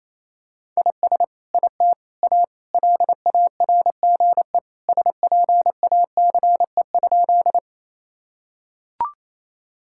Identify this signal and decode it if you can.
Morse code, heard as a keyed tone, about 28 wpm, 700 Hz. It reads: IS IT A LARGE SPACE?
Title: Sentences from Top 1000 Words - Review Part 5 - 28wpm